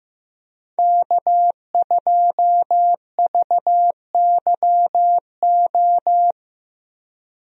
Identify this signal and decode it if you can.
Morse code, heard as a keyed tone, about 15 wpm, 700 Hz. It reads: K2VYO
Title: USA Call Signs - 5 Characters - Repeated 3x - 15wpm